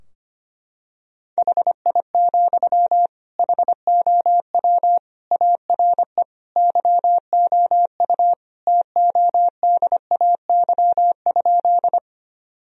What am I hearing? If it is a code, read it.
Morse code, heard as a keyed tone, about 25 wpm, 700 Hz. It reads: HI, HOW ARE YOU TODAY?
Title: Sentences from Top 1000 Words - Review Part 3 - 25wpm